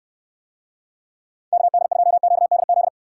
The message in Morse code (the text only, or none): KD2QDZ